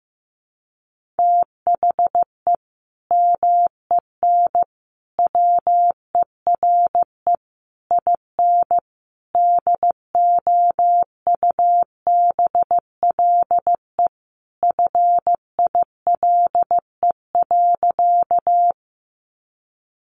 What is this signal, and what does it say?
Morse code, heard as a keyed tone, about 15 wpm, 700 Hz. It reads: THE MEN WERE IN DOUBLE FILE.
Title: Sentences from Top 1400 Words 15wpm